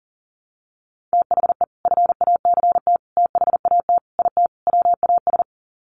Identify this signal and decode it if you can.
Morse code, heard as a keyed tone, about 40 wpm, 700 Hz. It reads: THE FACT THAT IT WAS
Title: Sets of 5 Words Encore2 40wpm